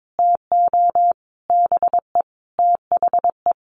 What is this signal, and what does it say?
Morse code, heard as a keyed tone, about 22 wpm, 700 Hz. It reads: TO BE THE
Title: Sets of 5 Words Encore 22wpm